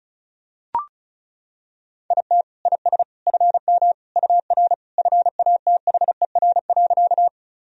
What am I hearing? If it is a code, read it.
Morse code, heard as a keyed tone, about 36 wpm, 700 Hz. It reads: IT IS FM UR FATHER.
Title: Sentences from Top 300 Words and Abbreviations 35wpm